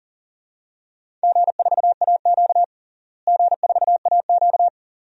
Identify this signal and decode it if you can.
Morse code, heard as a keyed tone, about 40 wpm, 700 Hz. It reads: G4AQ G4AQ
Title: DXCC-100 Call Signs - 4 Characters - Repeated 2x - 40wpm